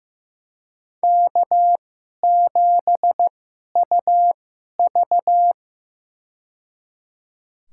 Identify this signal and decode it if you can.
Morse code, heard as a keyed tone, about 15 wpm, spaced as for 11 wpm, 700 Hz. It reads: K7UV
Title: USA Call Signs - 2x Character Spacing - 15wpm